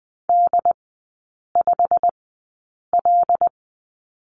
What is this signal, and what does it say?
Morse code, heard as a keyed tone, about 20 wpm, spaced as for 8 wpm, 700 Hz. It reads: D5L
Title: 087 Lesson 15e - Number 5 - Three Characters 20wpm